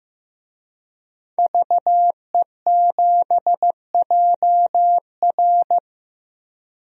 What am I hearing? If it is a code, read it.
Morse code, heard as a keyed tone, about 16 wpm, 700 Hz. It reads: VE7JR
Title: DXCC-100 Call Signs - 5 Characters - Repeated 2x - 15wpm